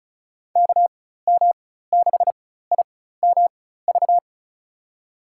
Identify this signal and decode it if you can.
Morse code, heard as a keyed tone, about 34 wpm, spaced as for 16 wpm, 700 Hz. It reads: KM6IMV
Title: USA Call Signs - 6 Characters - 4x Character Spacing - 35wpm